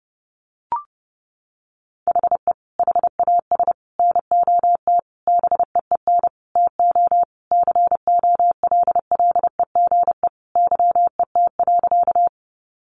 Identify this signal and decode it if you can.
Morse code, heard as a keyed tone, about 30 wpm, 700 Hz. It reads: HE HAS NOT BEEN TO COLLEGE YET.
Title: Sentences from Top 1300 Words 30wpm